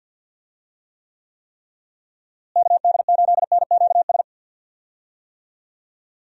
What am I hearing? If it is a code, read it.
Morse code, heard as a keyed tone, about 50 wpm, 700 Hz. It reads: KD8NQS
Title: USA Call Signs - 6 Characters - Repeated 3x - 50wpm